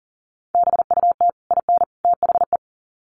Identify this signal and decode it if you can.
Morse code, heard as a keyed tone, about 39 wpm, 700 Hz. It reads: BUT IN THE
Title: Sets of 3 Words 40wpm